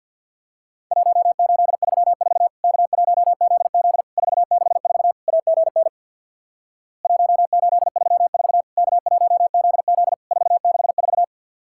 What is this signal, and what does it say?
Morse code, heard as a keyed tone, about 50 wpm, 700 Hz. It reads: 1834 K1ZZ 464 AGN 1834 K1ZZ 464
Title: Worked All Europe DX Contest - QTC Exchange with Repeats - 50wpm